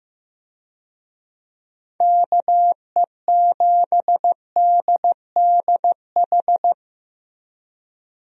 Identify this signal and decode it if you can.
Morse code, heard as a keyed tone, about 15 wpm, 700 Hz. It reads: KE7DDH
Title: USA Call Signs - 6 Characters - Repeated 3x - 15wpm